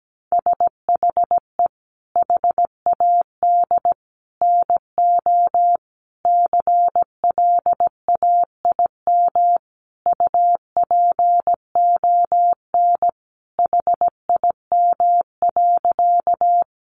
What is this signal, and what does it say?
Morse code, heard as a keyed tone, about 17 wpm, 700 Hz. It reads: SHE HAD NO CLAIM UPON HIM.
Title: Sentences from Top 1500 Words 17wpm